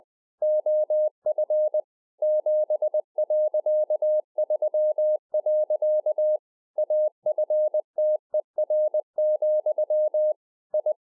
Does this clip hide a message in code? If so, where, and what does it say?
Yes: Morse code from 0.4 to 10.9 s, heard as a keyed tone, about 20 wpm, 610 Hz. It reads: OF 7.3. AFTER, I